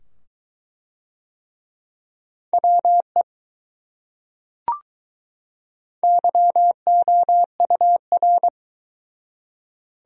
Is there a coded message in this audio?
Yes: Morse code from 2.5 to 8.5 s, heard as a keyed tone, about 22 wpm, 700 Hz. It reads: WE YOUR